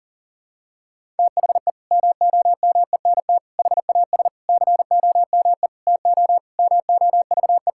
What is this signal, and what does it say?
Morse code, heard as a keyed tone, about 40 wpm, 700 Hz. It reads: THE MOMENT HAS COME TO MOVE
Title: Sentences from Top 1000 Words - Review Part 5 - 40wpm